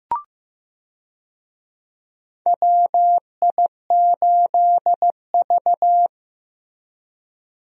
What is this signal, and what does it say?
Morse code, heard as a keyed tone, about 15 wpm, 700 Hz. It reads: WI8V
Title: USA Call Signs - 4 Characters - Repeated 3x - 15wpm